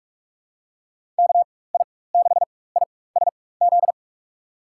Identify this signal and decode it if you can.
Morse code, heard as a keyed tone, about 45 wpm, spaced as for 21 wpm, 700 Hz. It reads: KI6ISZ
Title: USA Call Signs - 6 Characters - 4x Character Spacing - 45wpm